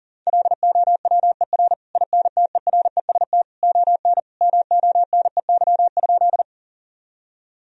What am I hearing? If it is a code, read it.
Morse code, heard as a keyed tone, about 40 wpm, 700 Hz. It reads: LOWER INTEREST ON MONEY?
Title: Sentences from Top 1000 Words - Review Part 4 - 40wpm